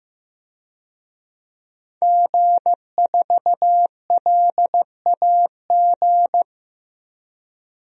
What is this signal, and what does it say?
Morse code, heard as a keyed tone, about 15 wpm, 700 Hz. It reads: G4LAG
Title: DXCC-100 Call Signs - 5 Characters - 15wpm